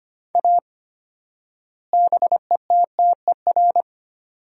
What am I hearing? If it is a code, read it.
Morse code, heard as a keyed tone, about 25 wpm, 700 Hz. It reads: A BETTER
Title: 4x Word-Spacing - Sentences from Top 400 Words - 25wpm